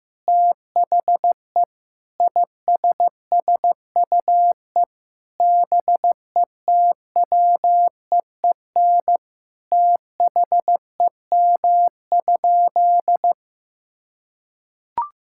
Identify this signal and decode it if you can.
Morse code, heard as a keyed tone, about 15 wpm, 700 Hz. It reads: THE ISSUE BETWEEN THEM?
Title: Sentences from Top 1000 Words - Review Part 6 - 15wpm